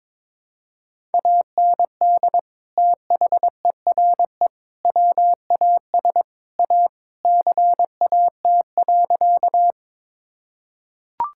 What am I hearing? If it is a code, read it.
Morse code, heard as a keyed tone, about 22 wpm, 700 Hz. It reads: AND THERE WAS A CAT.
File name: practice / 22wpm / Sentences with 4-Word Phrases - 22wpm